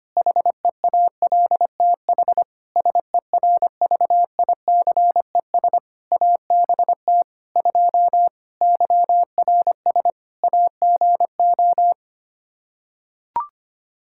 Morse code, heard as a keyed tone, about 25 wpm, 700 Hz. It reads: HEALTH SERVICES ABT 2 YRS AGO
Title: Practice QSO - Exchange 3 - More Info - 25wpm